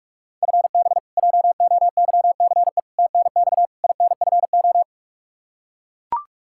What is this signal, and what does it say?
Morse code, heard as a keyed tone, about 42 wpm, 700 Hz. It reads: FB JOYCE TNX INFO
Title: K1USN SST QSO Practice 45wpm